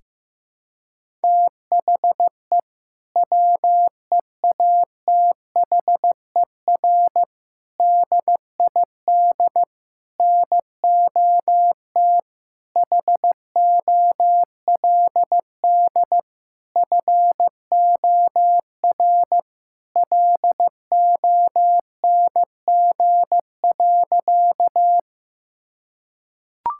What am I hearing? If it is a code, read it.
Morse code, heard as a keyed tone, about 15 wpm, 700 Hz. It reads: THE WEATHER DID NOT HOLD FOR LONG.
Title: Sentences from Top 1000 Words - Review Part 5 - 15wpm